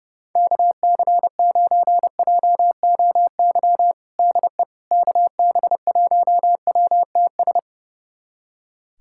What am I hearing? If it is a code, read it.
Morse code, heard as a keyed tone, about 30 wpm, 700 Hz. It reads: KC9JOY DE KB1WTS
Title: USA Callsign DE Callsign - 6 Characters - 30wpm